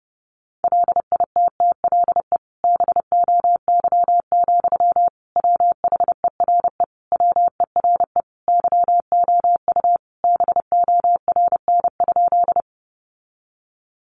Morse code, heard as a keyed tone, about 28 wpm, 700 Hz. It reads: LITTLE BOY, WHERE WERE YOU BORN?
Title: Sentences from Top 1000 Words 30wpm